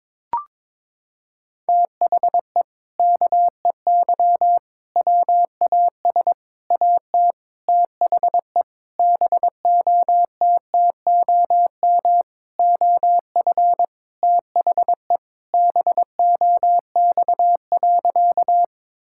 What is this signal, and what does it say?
Morse code, heard as a keyed tone, about 22 wpm, 700 Hz. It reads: THE KEY WAS AT THE BOTTOM OF THE BOX.
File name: practice / 22wpm / Sentences with 4-Word Phrases - 22wpm